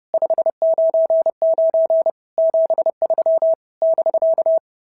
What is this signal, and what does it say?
Morse code, heard as a keyed tone, about 30 wpm, 650 Hz. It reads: 599 73 <BK>
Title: SOTA QSO Practice 30wpm